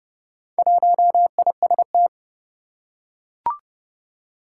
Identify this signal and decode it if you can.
Morse code, heard as a keyed tone, about 30 wpm, 700 Hz. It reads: 1IST
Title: Four Challenging Letter-Number Combinations 30wpm